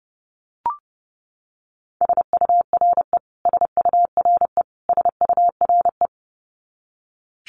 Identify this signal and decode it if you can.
Morse code, heard as a keyed tone, about 30 wpm, 700 Hz. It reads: SURE SURE SURE
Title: Top 500 Words Repeated 30wpm